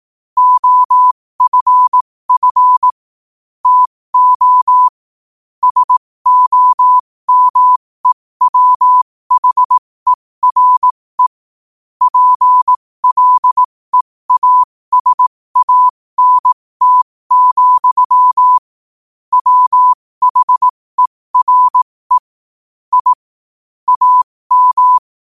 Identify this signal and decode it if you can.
Morse code, heard as a keyed tone, about 18 wpm, spaced as for 15 wpm, 1000 Hz. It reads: OFF TO SOMEWHERE PLEASANT, WHERE I AM
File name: 15wpm